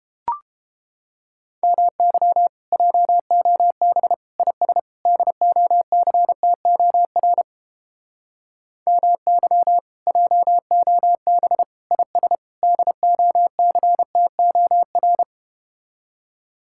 Morse code, heard as a keyed tone, about 30 wpm, 700 Hz. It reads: MY JOB IS DOCTOR MY JOB IS DOCTOR